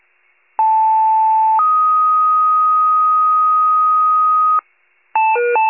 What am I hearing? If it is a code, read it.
Two-tone page, heard as a one-second tone, then a three-second tone, about 880 then 1250 Hz. It is heard from 0.6 to 4.6 s.
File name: wcalert